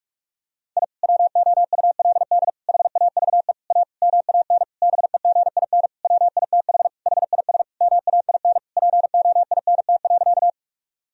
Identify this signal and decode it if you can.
Morse code, heard as a keyed tone, about 46 wpm, 700 Hz. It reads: I WOULD HAVE A MAN BEGIN WITH HIS MAIN POINT.
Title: Sentences from Top 1000 Words - Review Part 2 - 45wpm